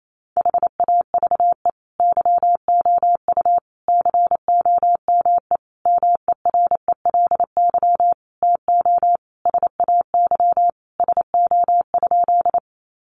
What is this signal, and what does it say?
Morse code, heard as a keyed tone, about 28 wpm, 700 Hz. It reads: HAVE YOU COME MERELY TO SAY SO?
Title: Sentences from Top 1400 Words 28wpm